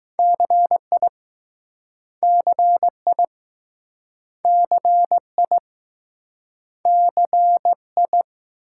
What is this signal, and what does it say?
Morse code, heard as a keyed tone, about 20 wpm, 700 Hz. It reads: CI CI CI CI